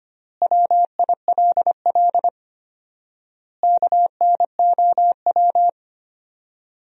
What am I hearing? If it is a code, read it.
Morse code, heard as a keyed tone, about 25 wpm, 700 Hz. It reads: WILL KNOW
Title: 4x Word-Spacing - Sentences from Top 400 Words - 25wpm